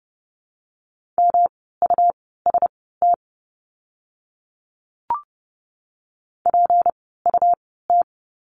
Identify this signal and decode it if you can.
Morse code, heard as a keyed tone, about 30 wpm, spaced as for 17 wpm, 700 Hz. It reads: MUST PUT
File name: course / 30:10wpm / 237 Lesson 38f - Letter q - Words 30wpm